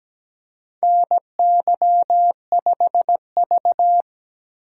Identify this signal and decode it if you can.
Morse code, heard as a keyed tone, about 17 wpm, 700 Hz. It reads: NY5V
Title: USA Call Signs - 4 Characters - 8x Character Spacing - 17wpm